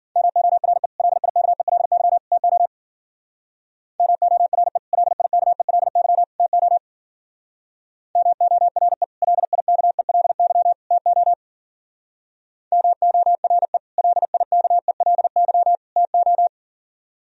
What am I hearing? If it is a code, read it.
Morse code, heard as a keyed tone, about 52 wpm, 700 Hz. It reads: MORE LIKELY TO MORE LIKELY TO MORE LIKELY TO MORE LIKELY TO